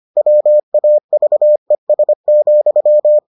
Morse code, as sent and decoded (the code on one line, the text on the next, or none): .-- .- ...- . ... --..--
WAVES,